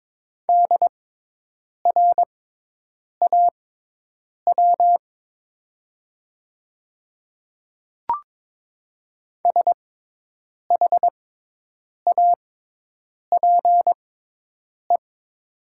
Morse code, heard as a keyed tone, about 22 wpm, spaced as for 8 wpm, 700 Hz. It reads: DRAW SHAPE
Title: 6x Character Spacing - Top 500 Words - 22wpm